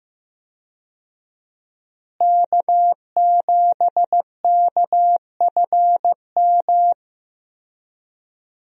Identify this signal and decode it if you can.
Morse code, heard as a keyed tone, about 16 wpm, 700 Hz. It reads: K7KFM
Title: USA Call Signs - 5 Characters - Repeated 2x - 15wpm